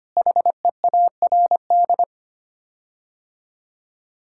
Morse code, heard as a keyed tone, about 25 wpm, 700 Hz. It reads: HEARD
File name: Sets of 9 Words - 25wpm